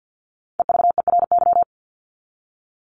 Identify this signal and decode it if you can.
Morse code, heard as a keyed tone, about 50 wpm, 700 Hz. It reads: EVERY
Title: Top 200 Words 50wpm